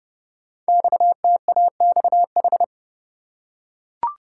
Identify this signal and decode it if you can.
Morse code, heard as a keyed tone, about 28 wpm, 700 Hz. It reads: XTAXH
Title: Code Groups - 5 Letters - 30wpm